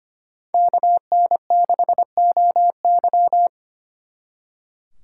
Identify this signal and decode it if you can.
Morse code, heard as a keyed tone, about 32 wpm, 700 Hz. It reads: KN6OY